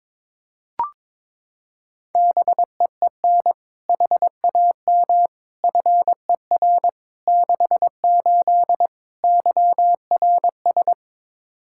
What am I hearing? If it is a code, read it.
Morse code, heard as a keyed tone, about 22 wpm, 700 Hz. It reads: BEEN HAM FER 68 YRS
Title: K1USN SST QSO Practice 22wpm